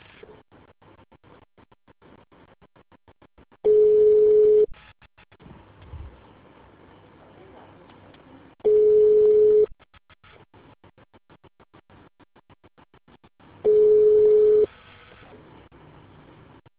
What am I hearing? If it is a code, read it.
Ringback tone (Europe, 425 Hz).